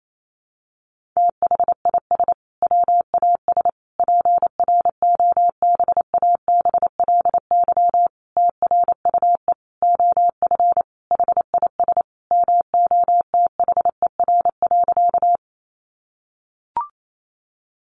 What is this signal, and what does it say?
Morse code, heard as a keyed tone, about 28 wpm, 700 Hz. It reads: THIS WAS PROBABLY TRUE OF HIS MOTHER.
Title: Sentences from Top 700 Words 28wpm